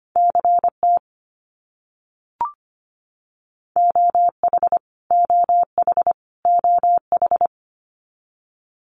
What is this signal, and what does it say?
Morse code, heard as a keyed tone, about 25 wpm, 700 Hz. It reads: CT OH OH OH
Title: US States Repeated 3x 25wpm